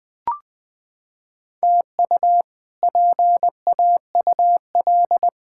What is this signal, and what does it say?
Morse code, heard as a keyed tone, about 20 wpm, 700 Hz. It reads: TU PAUL
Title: K1USN SST QSO Practice - 20wpm